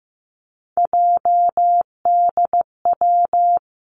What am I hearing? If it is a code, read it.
Morse code, heard as a keyed tone, about 15 wpm, 700 Hz. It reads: JDW
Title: Three Challenging Letter-Number Combinations 15wpm